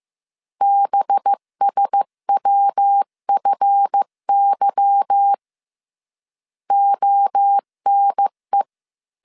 Morse code, heard as a keyed tone, about 15 wpm, 800 Hz. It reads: BSWFY ONE
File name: CW / DET / 1 9 1500C S B 14wpm 3302